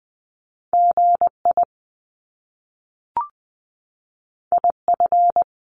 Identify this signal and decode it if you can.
Morse code, heard as a keyed tone, about 20 wpm, 700 Hz. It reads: GI IF
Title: Top Two Letter Combinations in Words 20wpm